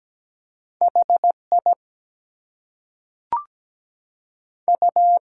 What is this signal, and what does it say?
Morse code, heard as a keyed tone, about 17 wpm, 700 Hz. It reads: HI U
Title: Two Letter Words - 17wpm